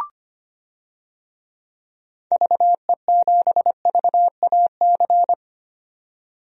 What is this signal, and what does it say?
Morse code, heard as a keyed tone, about 25 wpm, 700 Hz. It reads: VE7VAC
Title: DXCC-100 Call Signs - 6 Characters - 25wpm